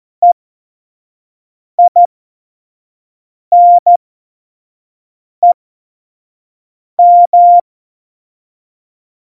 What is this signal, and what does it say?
Morse code, heard as a keyed tone, about 14 wpm, spaced as for 5 wpm, 700 Hz. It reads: EINEM
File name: Morsekurs / Witze / Witz 07. Januar 2020, 25 F70